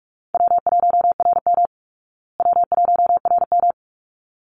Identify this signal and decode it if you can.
Morse code, heard as a keyed tone, about 44 wpm, 700 Hz. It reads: W1RM W1RM
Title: Easy Call Signs Repeated 2x - 45wpm